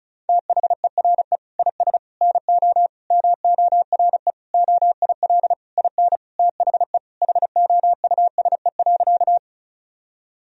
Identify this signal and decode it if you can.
Morse code, heard as a keyed tone, about 36 wpm, 700 Hz. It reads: THERE IS NO MORE OIL IN THE HOUSE.
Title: Sentences from Top 1000 Words - Review Part 4 - 35wpm